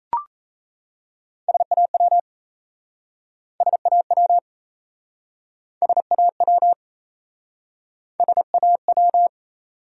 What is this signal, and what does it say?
Morse code, heard as a keyed tone, about 37 wpm, 700 Hz. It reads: SAW SAW SAW SAW